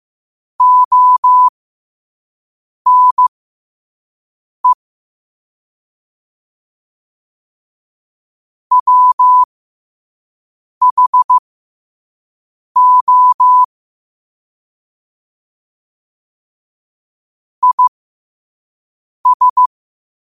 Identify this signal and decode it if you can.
Morse code, heard as a keyed tone, about 15 wpm, spaced as for 5 wpm, 1000 Hz. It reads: ONE WHO IS
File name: 5wpm